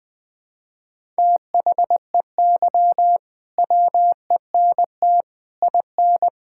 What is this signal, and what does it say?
Morse code, heard as a keyed tone, about 20 wpm, 700 Hz. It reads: THEY WENT IN
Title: Sentences from Top 1000 Words - Review Part 5 - 20wpm